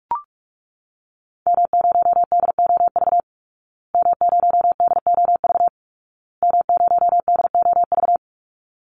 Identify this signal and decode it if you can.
Morse code, heard as a keyed tone, about 44 wpm, 700 Hz. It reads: M0DOV M0DOV M0DOV
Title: Call Signs Repeated 2x - 45wpm